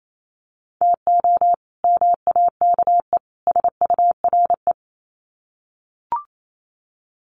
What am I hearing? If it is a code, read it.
Morse code, heard as a keyed tone, about 28 wpm, 700 Hz. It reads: TO MAKE SURE
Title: Sets of 3 Words 28wpm